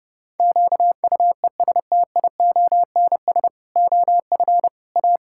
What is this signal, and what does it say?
Morse code, heard as a keyed tone, about 30 wpm, 700 Hz. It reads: QUESTIONS OF A
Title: Pangram Challenge v2 - 30wpm